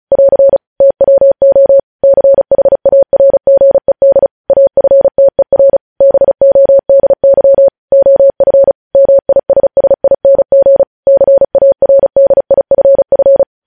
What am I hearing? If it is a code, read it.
Morse code, heard as a keyed tone, about 36 wpm, 550 Hz. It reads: <AR> TWO CHARGED AFTER BODY OF MISSING CARDIFF
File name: podcast / News headlines-1x-35x35